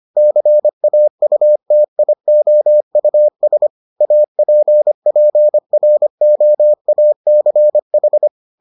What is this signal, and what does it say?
Morse code, heard as a keyed tone, about 25 wpm, 600 Hz. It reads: CAUTIOUS APPROACH